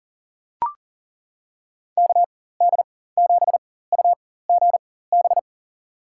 Morse code, heard as a keyed tone, about 40 wpm, spaced as for 19 wpm, 700 Hz. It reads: KD7UGB
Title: USA Call Signs - 6 Characters - 4x Character Spacing - 40wpm